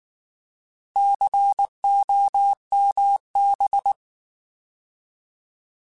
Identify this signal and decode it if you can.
Morse code, heard as a keyed tone, about 19 wpm, 800 Hz. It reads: COMB
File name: Alphabetical / Alphabetical mp3s / Cs @18